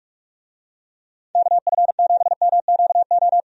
Morse code, heard as a keyed tone, about 44 wpm, 700 Hz. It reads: KF7MQO
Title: USA Call Signs - 6 Characters - 45wpm